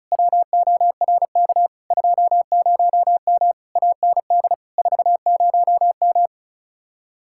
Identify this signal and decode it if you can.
Morse code, heard as a keyed tone, about 44 wpm, 700 Hz. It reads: WORK 20M AND 40M
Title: Speed-Racing Common QSO Phrases Encore - 35wpm